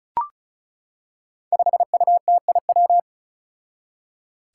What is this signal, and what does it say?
Morse code, heard as a keyed tone, about 34 wpm, 700 Hz. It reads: 5UTIW
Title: Five Challenging Letter-Number Combinations 35wpm